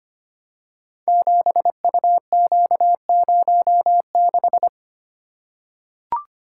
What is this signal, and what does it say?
Morse code, heard as a keyed tone, about 25 wpm, 700 Hz. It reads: 7UQ06
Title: Code Groups - 5 Letter:Numbers - 25wpm